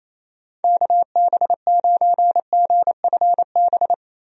Morse code, heard as a keyed tone, about 27 wpm, 700 Hz. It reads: KB9GFB
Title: USA Call Signs - 6 Characters - Repeated 2x - 28wpm